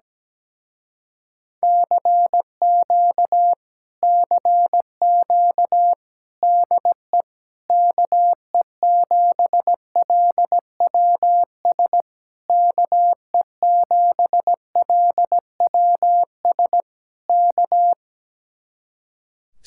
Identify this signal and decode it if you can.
Morse code, heard as a keyed tone, about 17 wpm, 700 Hz. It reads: CQ CQ DE KE7LWS KE7LWS K
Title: Speed-Racing Common QSO Phrases Encore - 17wpm